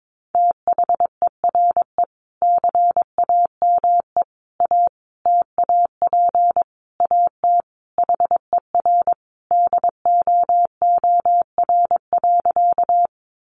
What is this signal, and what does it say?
Morse code, heard as a keyed tone, about 22 wpm, 700 Hz. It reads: THERE CAME A TAP AT HER DOOR.
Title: Sentences from Top 1400 Words 22wpm